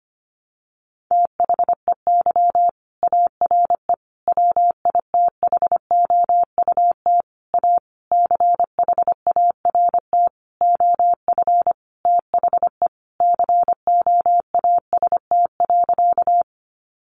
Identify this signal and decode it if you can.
Morse code, heard as a keyed tone, about 25 wpm, 700 Hz. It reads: THEY ARE WITHOUT A CHART OF THE COAST.
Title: Sentences from Top 1300 Words 25wpm